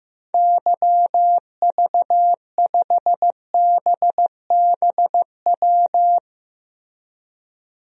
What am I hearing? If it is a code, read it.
Morse code, heard as a keyed tone, about 15 wpm, 700 Hz. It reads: YV5BBW
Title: DXCC-100 Call Signs - 6 Characters - Repeated 3x - 15wpm